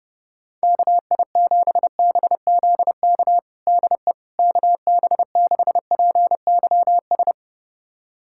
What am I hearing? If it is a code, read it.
Morse code, heard as a keyed tone, about 30 wpm, 700 Hz. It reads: KI7BZK DE KB6PYS
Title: USA Callsign DE Callsign - 6 Characters - 30wpm